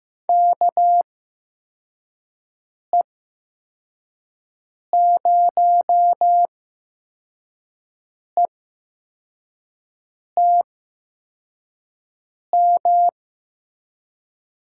Morse code, heard as a keyed tone, about 15 wpm, spaced as for 4 wpm, 700 Hz. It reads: KE0ETM
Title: USA Call Signs - 8x Character Spacing - 15wpm